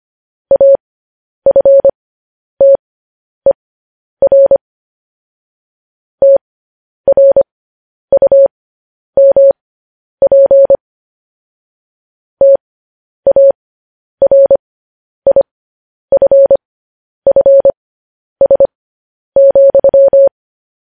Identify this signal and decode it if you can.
Morse code, heard as a keyed tone, about 25 wpm, spaced as for 10 wpm, 550 Hz. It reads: AFTER TRUMP TARIFFS,
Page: news headlines in morse code at 10 wpm